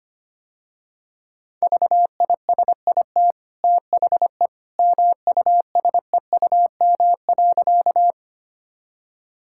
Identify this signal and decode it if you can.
Morse code, heard as a keyed tone, about 25 wpm, 700 Hz. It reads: VISIT THE MUSEUM.
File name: Easy Sentences with Hints - 25wpm